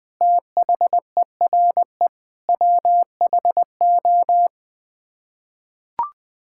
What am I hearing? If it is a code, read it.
Morse code, heard as a keyed tone, about 20 wpm, 700 Hz. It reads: THERE WHO
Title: Sets of 9 Words - 20wpm